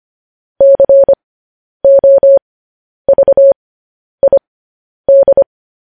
Morse code, heard as a keyed tone, about 25 wpm, spaced as for 10 wpm, 550 Hz. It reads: COVID
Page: news headlines in morse code at 10 wpm